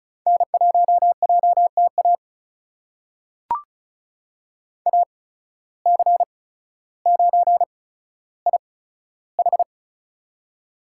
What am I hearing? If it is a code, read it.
Morse code, heard as a keyed tone, about 36 wpm, spaced as for 15 wpm, 700 Hz. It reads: N1JTA AC9IH
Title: USA Call Signs - 5 Characters - 8x Character Spacing - 35wpm